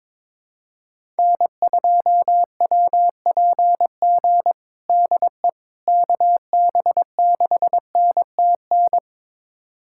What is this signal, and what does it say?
Morse code, heard as a keyed tone, about 22 wpm, 700 Hz. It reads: N2WPG DE KB6NTN
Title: USA Callsign DE Callsign 22wpm